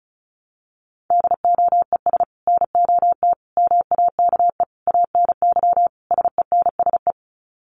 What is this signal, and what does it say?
Morse code, heard as a keyed tone, about 34 wpm, 700 Hz. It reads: DOES NOT MAKE ANY SENSE
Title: Sets of 5 Words Encore2 35wpm